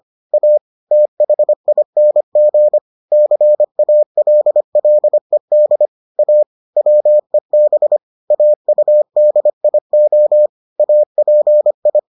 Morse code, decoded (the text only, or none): A THING CALLED A WEB AUDIO API